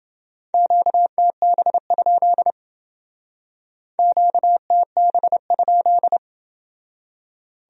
Morse code, heard as a keyed tone, about 29 wpm, 700 Hz. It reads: QTB? QTB?